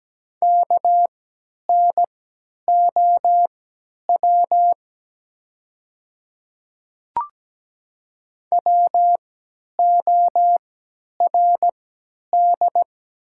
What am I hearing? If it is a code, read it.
Morse code, heard as a keyed tone, about 17 wpm, spaced as for 10 wpm, 700 Hz. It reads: KNOW WORD
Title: 3x Character Spacing - Top 100 Words - 17wpm